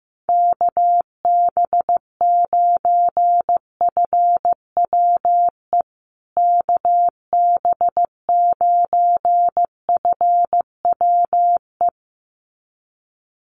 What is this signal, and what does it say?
Morse code, heard as a keyed tone, about 15 wpm, 700 Hz. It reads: KB9FWE KB9FWE
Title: USA Call Signs Repeated 2x 15wpm